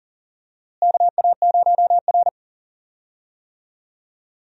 Morse code, heard as a keyed tone, about 40 wpm, 700 Hz. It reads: KA0R
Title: USA Call Signs - 4 Characters - Repeated 3x - 40wpm